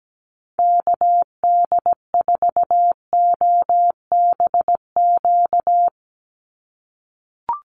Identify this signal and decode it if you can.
Morse code, heard as a keyed tone, about 17 wpm, 700 Hz. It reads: KD4OBQ
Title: USA Call Signs 17wpm